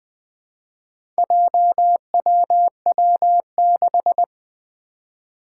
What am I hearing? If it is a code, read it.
Morse code, heard as a keyed tone, about 20 wpm, 700 Hz. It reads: JWW6
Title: Four Challenging Letter-Number Combinations 20wpm